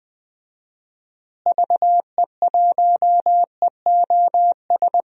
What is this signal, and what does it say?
Morse code, heard as a keyed tone, about 20 wpm, 700 Hz. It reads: VE1EOS
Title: DXCC-100 Call Signs - 6 Characters - Repeated 3x - 20wpm